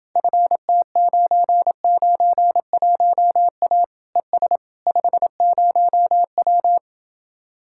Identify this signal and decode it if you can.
Morse code, heard as a keyed tone, about 29 wpm, 700 Hz. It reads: FT991A ES 50W